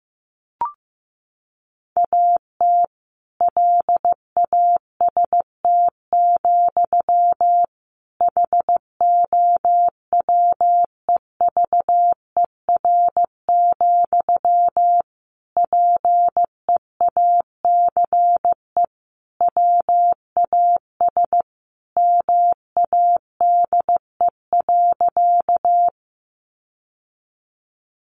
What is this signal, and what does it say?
Morse code, heard as a keyed tone, about 15 wpm, 700 Hz. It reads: AT LAST, HOWEVER, PEACE WAS MADE.
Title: Sentences from Top 1500 Words 15wpm